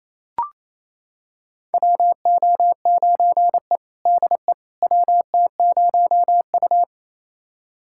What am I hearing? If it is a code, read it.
Morse code, heard as a keyed tone, about 28 wpm, 700 Hz. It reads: WO9E DE WT0U
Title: USA Callsign DE Callsign - 4 Characters - 28wpm